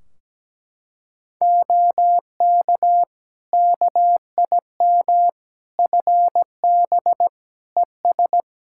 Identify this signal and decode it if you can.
Morse code, heard as a keyed tone, about 17 wpm, 700 Hz. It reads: OK KIM FB ES
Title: Practice QSO - Exchange 2 - Info - 17wpm